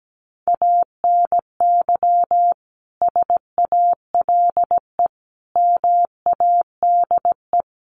Morse code, heard as a keyed tone, about 17 wpm, 700 Hz. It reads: ANY SALE MADE
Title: Sentences from Top 1400 Words 17wpm